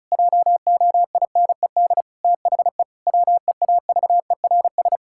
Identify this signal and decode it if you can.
Morse code, heard as a keyed tone, about 36 wpm, 700 Hz. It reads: JOINED THE WEAVERS
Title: Pangram Challenge v2 - 35wpm